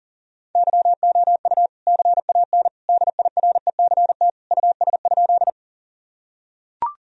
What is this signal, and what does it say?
Morse code, heard as a keyed tone, about 40 wpm, 700 Hz. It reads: YOU CAN DIRECT US?
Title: Sentences from Top 1000 Words - Review Part 2 - 40wpm